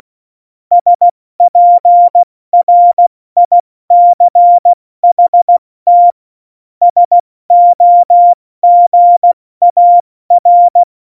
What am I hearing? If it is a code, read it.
Morse code, heard as a keyed tone, about 16 wpm, spaced as for 14 wpm, 700 Hz. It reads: SPRICHT SOGAR
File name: Morsekurs / Witze / Witz 20. November 2018, 70 F80